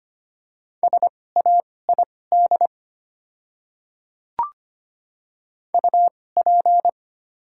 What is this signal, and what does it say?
Morse code, heard as a keyed tone, about 25 wpm, spaced as for 18 wpm, 700 Hz. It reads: SAID UP
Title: 2x Character Spacing - Top 100 Words - 25wpm